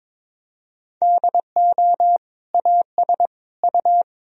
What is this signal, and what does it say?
Morse code, heard as a keyed tone, about 22 wpm, 700 Hz. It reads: DO AS U
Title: Sentences from Top 100 Words and Abbreviations 22wpm